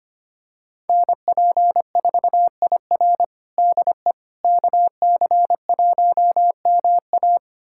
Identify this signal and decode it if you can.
Morse code, heard as a keyed tone, about 25 wpm, 700 Hz. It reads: NP4IR DE KC1MA
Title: USA Callsign DE Callsign - 5 Characters - 25wpm